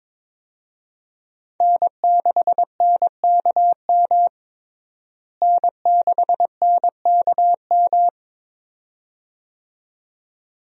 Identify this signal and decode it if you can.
Morse code, heard as a keyed tone, about 22 wpm, 700 Hz. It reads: N6NKM N6NKM
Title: USA Call Signs - 5 Characters - Repeated 2x - 22wpm